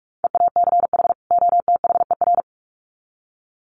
Morse code, heard as a keyed tone, about 44 wpm, 700 Hz. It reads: EACH OTHER
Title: Sets of 2 Words 45wpm